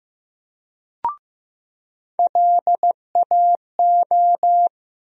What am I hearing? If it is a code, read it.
Morse code, heard as a keyed tone, about 15 wpm, 700 Hz. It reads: LAO
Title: USA Call Sign Postfixes - 3 Letters 15wpm